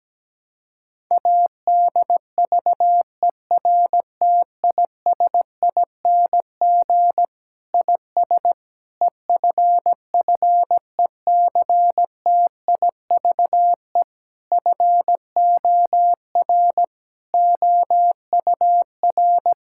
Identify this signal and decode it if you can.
Morse code, heard as a keyed tone, about 17 wpm, 700 Hz. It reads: ADVERTISING IS EFFECTIVE FOR OUR
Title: Sentences from Top 1000 Words - Review Part 6 - 17wpm